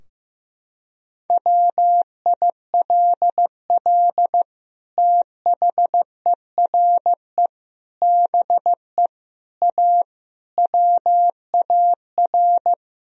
Morse code, heard as a keyed tone, about 15 wpm, 700 Hz. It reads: WILL THERE BE A WAR
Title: Sentences from Top 1000 Words - Review Part 2 - 15wpm